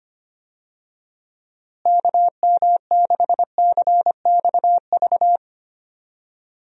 Morse code, heard as a keyed tone, about 25 wpm, 700 Hz. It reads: KM6CXV
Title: USA Call Signs - 6 Characters - Repeated 3x - 25wpm